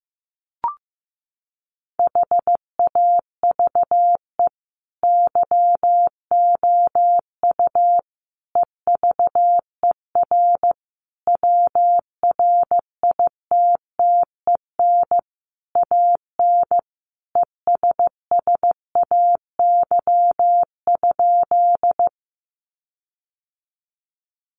Morse code, heard as a keyed tone, about 15 wpm, 700 Hz. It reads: HAVE YOU EVER WRITTEN AN ESSAY?
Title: Sentences from Top 1500 Words 15wpm